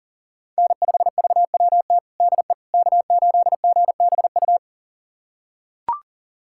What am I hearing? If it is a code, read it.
Morse code, heard as a keyed tone, about 37 wpm, 700 Hz. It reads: N5VWT DE K8GBU
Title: USA Callsign DE Callsign - 5 Characters - 40wpm